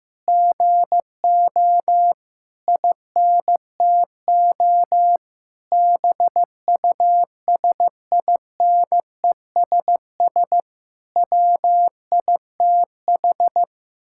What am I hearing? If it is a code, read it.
Morse code, heard as a keyed tone, about 15 wpm, 700 Hz. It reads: GO INTO BUSINESS WITH
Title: Sentences from Top 1000 Words - Review Part 6 - 15wpm